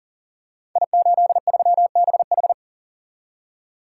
Morse code, heard as a keyed tone, about 38 wpm, 700 Hz. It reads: I83BH
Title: Code Groups - 5 Letter:Numbers - 40wpm